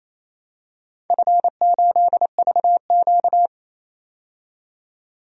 Morse code, heard as a keyed tone, about 28 wpm, 700 Hz. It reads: F8VQ
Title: DXCC-100 Call Signs - 4 Characters - 28wpm